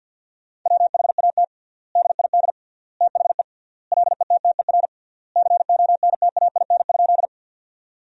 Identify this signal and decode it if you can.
Morse code, heard as a keyed tone, about 50 wpm, 700 Hz. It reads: WHAT DID THE LETTER CONTAIN?
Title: 3x Word-Spacing - Sentences from Top 500 Words - 50wpm